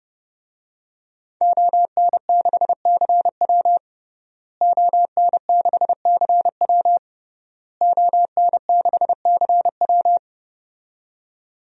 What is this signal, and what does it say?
Morse code, heard as a keyed tone, about 30 wpm, 700 Hz. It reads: ON6CW ON6CW ON6CW
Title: DXCC-100 Call Signs - 5 Characters - Repeated 3x - 30wpm